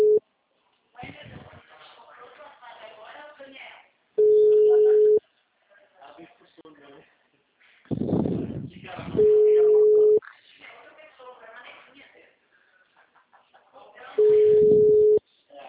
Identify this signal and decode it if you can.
Ringback tone (Europe, 425 Hz), with speech.
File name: out- -2008- - - .5742